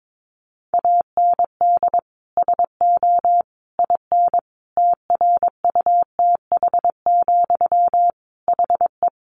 Morse code, heard as a keyed tone, about 22 wpm, 700 Hz. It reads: AND SO IN TRUTH, HE
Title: Sentences from Top 800 Words 22wpm